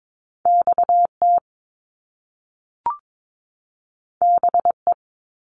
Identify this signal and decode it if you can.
Morse code, heard as a keyed tone, about 22 wpm, 700 Hz. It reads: XT BE
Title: Top Two Letter Combinations in Words 22wpm